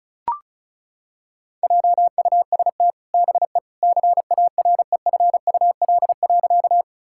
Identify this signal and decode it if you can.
Morse code, heard as a keyed tone, about 36 wpm, 700 Hz. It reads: JUST BE CAREFUL.
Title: Sentences from Top 1600 Words 35wpm